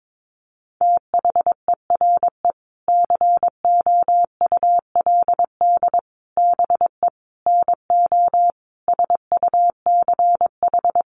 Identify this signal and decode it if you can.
Morse code, heard as a keyed tone, about 22 wpm, 700 Hz. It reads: THERE COULD BE NO SUCH
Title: Sentences from Top 600 Words 22wpm